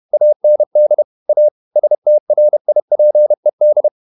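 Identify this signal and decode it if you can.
Morse code, heard as a keyed tone, about 31 wpm, 600 Hz. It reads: AND A STRIPED
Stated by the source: Morse code at 30 wpm, the novel Never by Ken Follett in English, from CW med Gnister.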